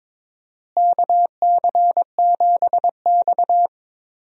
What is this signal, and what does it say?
Morse code, heard as a keyed tone, about 22 wpm, 700 Hz. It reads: KC7X
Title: USA Call Signs - 4 Characters - Repeated 3x - 22wpm